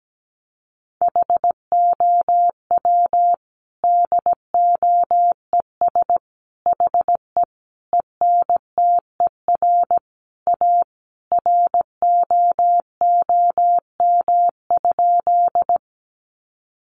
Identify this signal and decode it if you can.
Morse code, heard as a keyed tone, about 17 wpm, 700 Hz. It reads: HOW DOES HE ENTER A ROOM?
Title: Sentences from Top 1300 Words 17wpm